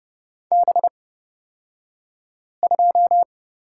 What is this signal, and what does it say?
Morse code, heard as a keyed tone, about 29 wpm, spaced as for 5 wpm, 700 Hz. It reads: B2
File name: Lesson 38d - Letter q - Two Characters 30wpm